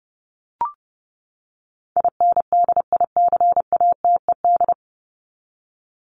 Morse code, heard as a keyed tone, about 30 wpm, 700 Hz. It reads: INDICATED
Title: ED Suffix - Top 100 Words with Base Words 30wpm